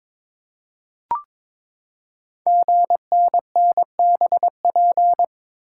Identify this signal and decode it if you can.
Morse code, heard as a keyed tone, about 22 wpm, 700 Hz. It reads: GNNBP
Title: Code Groups - 5 Letters - 22wpm